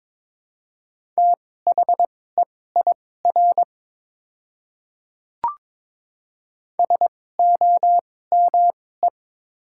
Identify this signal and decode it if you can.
Morse code, heard as a keyed tone, about 22 wpm, spaced as for 16 wpm, 690 Hz. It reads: THEIR SOME
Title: 2x Character Spacing - Top 200 Words - 22wpm